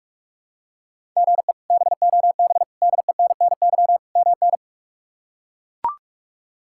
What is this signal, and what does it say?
Morse code, heard as a keyed tone, about 42 wpm, 700 Hz. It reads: GE BOB DENNY MN
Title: K1USN SST QSO Practice - 45wpm